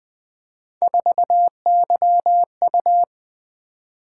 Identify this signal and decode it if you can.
Morse code, heard as a keyed tone, about 20 wpm, 700 Hz. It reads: 4YU
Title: Three Letter-Number Combinations - 20wpm